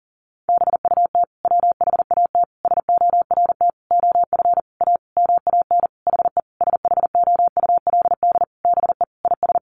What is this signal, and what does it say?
Morse code, heard as a keyed tone, about 40 wpm, 700 Hz. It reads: BUT WHAT SORT OF A MAN HE SHOULD BE IS
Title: Sentences from Top 1300 Words 40wpm